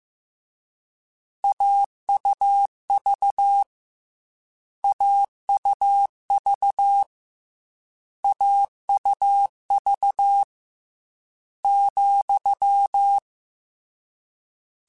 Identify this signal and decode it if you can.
Morse code, heard as a keyed tone, about 15 wpm, 800 Hz. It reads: AUV AUV AUV ,
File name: Difficult Combos / Difficult Combos-mp3s / AUV AUV4 @13